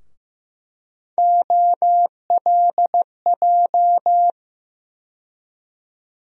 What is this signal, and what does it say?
Morse code, heard as a keyed tone, about 15 wpm, 700 Hz. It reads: OLJ
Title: USA Call Sign Postfixes - 3 Letters 15wpm